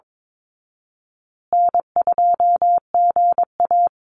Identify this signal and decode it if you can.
Morse code, heard as a keyed tone, about 22 wpm, 700 Hz. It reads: N2GA
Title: Easy Call Signs Repeated 3x - 22wpm